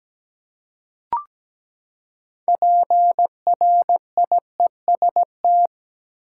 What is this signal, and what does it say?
Morse code, heard as a keyed tone, about 17 wpm, 700 Hz. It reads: PRIEST
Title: Six Letter Words - 17wpm